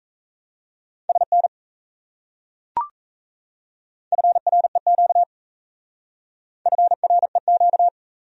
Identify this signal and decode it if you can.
Morse code, heard as a keyed tone, about 35 wpm, 700 Hz. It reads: SN FREQ FREQ
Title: Speed-Racing Abbreviations - 28wpm